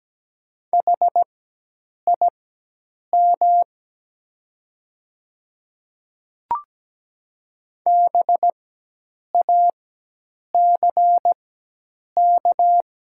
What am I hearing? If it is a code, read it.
Morse code, heard as a keyed tone, about 17 wpm, spaced as for 8 wpm, 690 Hz. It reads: HIM BACK
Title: 4x Character Spacing - Top 300 Words - 17wpm